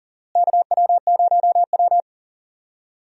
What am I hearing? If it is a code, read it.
Morse code, heard as a keyed tone, about 38 wpm, 700 Hz. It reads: KW0W
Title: USA Call Signs - 4 Characters - Repeated 2x - 40wpm